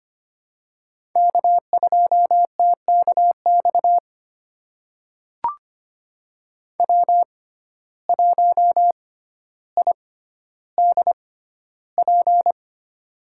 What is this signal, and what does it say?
Morse code, heard as a keyed tone, about 25 wpm, spaced as for 13 wpm, 700 Hz. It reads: K2TKX W1IDP
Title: USA Call Signs - 5 Characters - 6x Character Spacing - 25wpm